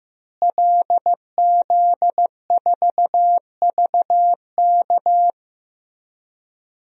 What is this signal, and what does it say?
Morse code, heard as a keyed tone, about 15 wpm, 700 Hz. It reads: LZ4VK
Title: DXCC-100 Call Signs - 5 Characters - Repeated 3x - 15wpm